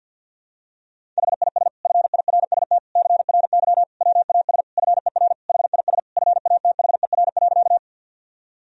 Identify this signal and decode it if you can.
Morse code, heard as a keyed tone, about 52 wpm, 700 Hz. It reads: HIS FIRST CRY WAS FER HIS FATHER.
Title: Sentences from Top 500 Words and Abbreviations 50wpm